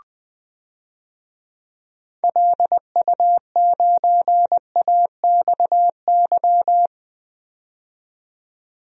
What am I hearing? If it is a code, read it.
Morse code, heard as a keyed tone, about 20 wpm, 700 Hz. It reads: LU9AXY
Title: DXCC-100 Call Signs - 6 Characters - 20wpm